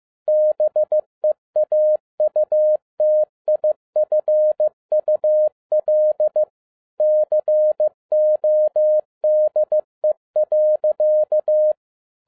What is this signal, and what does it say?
Morse code, heard as a keyed tone, about 15 wpm, 610 Hz. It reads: BEAUTIFUL CODE.